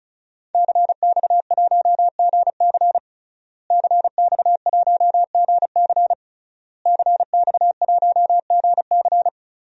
Morse code, heard as a keyed tone, about 36 wpm, 700 Hz. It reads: CX1GC CX1GC CX1GC
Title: DXCC-100 Call Signs - 5 Characters - Repeated 3x - 35wpm